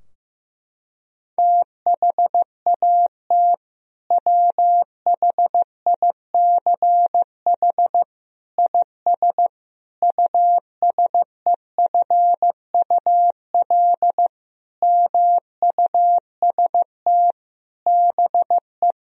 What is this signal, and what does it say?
Morse code, heard as a keyed tone, about 15 wpm, 700 Hz. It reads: THAT WHICH IS USEFUL MUST BE
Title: Sentences from Top 1000 Words - Review Part 4 - 15wpm